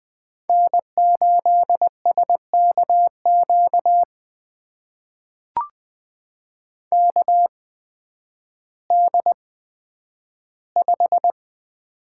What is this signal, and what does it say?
Morse code, heard as a keyed tone, about 20 wpm, spaced as for 11 wpm, 700 Hz. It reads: N8SKQ KD5
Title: USA Call Signs - 5 Characters - 8x Character Spacing - 20wpm